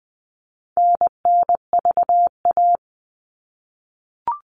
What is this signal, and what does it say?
Morse code, heard as a keyed tone, about 20 wpm, 700 Hz. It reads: NNVA
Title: All Cut Numbers - 4-Digit 20wpm